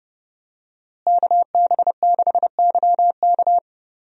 Morse code, heard as a keyed tone, about 30 wpm, 700 Hz. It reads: KB6YK
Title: USA Call Signs - 5 Characters - Repeated 3x - 30wpm